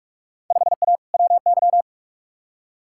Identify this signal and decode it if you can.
Morse code, heard as a keyed tone, about 48 wpm, 700 Hz. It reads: 5A WY
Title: Field Day Exchanges Encore - 45wpm